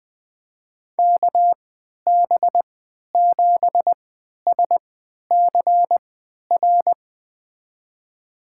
Morse code, heard as a keyed tone, about 21 wpm, spaced as for 11 wpm, 700 Hz. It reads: KB7SCR
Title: USA Call Signs - 3x Character Spacing - 20wpm